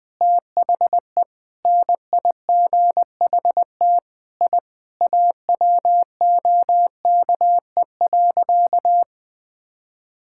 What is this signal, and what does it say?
Morse code, heard as a keyed tone, about 20 wpm, 700 Hz. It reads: THE NIGHT I AWOKE.
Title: Sentences with 4-Word Phrases - Spoken Beginning - 20wpm